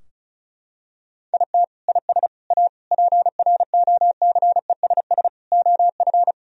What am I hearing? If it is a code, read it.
Morse code, heard as a keyed tone, about 36 wpm, 700 Hz. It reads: IT IS A PROCESS OF
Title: Sentences from Top 1000 Words - Review Part 4 - 35wpm